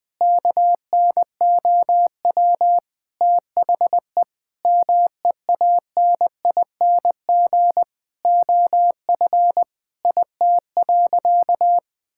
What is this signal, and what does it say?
Morse code, heard as a keyed tone, about 20 wpm, 700 Hz. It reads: KNOW THE MEANING OF IT.
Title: Sentences from Top 1000 Words - Review Part 6 - 20wpm